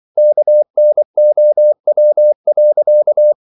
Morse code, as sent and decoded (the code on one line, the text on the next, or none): -.- -. --- .-- .-.-.-
KNOW.